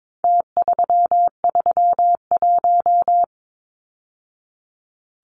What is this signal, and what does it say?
Morse code, heard as a keyed tone, about 22 wpm, 700 Hz. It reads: T331
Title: Numbers - Contest Style - 4-Digit 22wpm